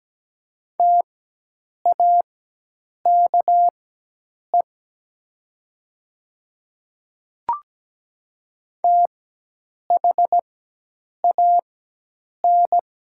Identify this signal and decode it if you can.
Morse code, heard as a keyed tone, about 17 wpm, spaced as for 8 wpm, 700 Hz. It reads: TAKE THAN
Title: 4x Character Spacing - Top 500 Words - 17wpm